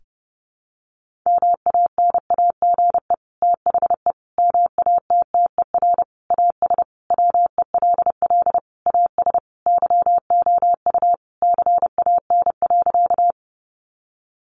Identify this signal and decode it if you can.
Morse code, heard as a keyed tone, about 30 wpm, 700 Hz. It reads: MANAGE THE MATTER AS WELL AS YOU CAN.
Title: Sentences from Top 1400 Words 30wpm